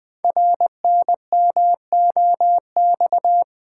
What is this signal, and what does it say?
Morse code, heard as a keyed tone, about 20 wpm, 700 Hz. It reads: RNMOX
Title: Code Groups - 5 Letters - 20wpm